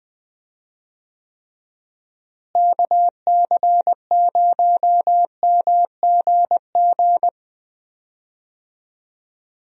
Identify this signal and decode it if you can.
Morse code, heard as a keyed tone, about 20 wpm, 700 Hz. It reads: KC0MGG
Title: USA Call Signs - 6 Characters - Repeated 2x - 20wpm